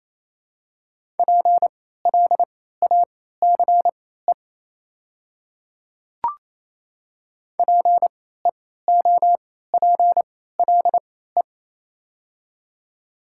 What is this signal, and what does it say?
Morse code, heard as a keyed tone, about 28 wpm, spaced as for 16 wpm, 700 Hz. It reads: PLACE PEOPLE
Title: 3x Character Spacing - Top 300 Words - 28wpm